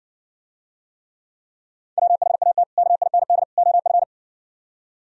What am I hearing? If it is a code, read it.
Morse code, heard as a keyed tone, about 59 wpm, 700 Hz. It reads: WHAT KIND OF